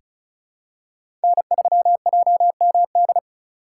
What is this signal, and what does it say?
Morse code, heard as a keyed tone, about 34 wpm, 700 Hz. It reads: N3JMD
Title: USA Call Signs - 5 Characters - 4x Character Spacing - 35wpm